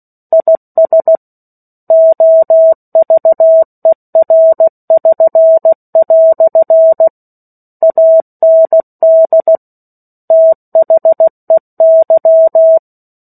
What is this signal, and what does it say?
Morse code, heard as a keyed tone, about 16 wpm, 650 Hz. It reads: IS OVER<SN>" AND THEY
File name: 03-Chapter-03 16 WPM